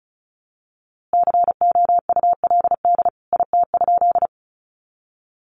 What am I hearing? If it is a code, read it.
Morse code, heard as a keyed tone, about 34 wpm, 700 Hz. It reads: COULD IT?
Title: A Sound Of Thunder 35wpm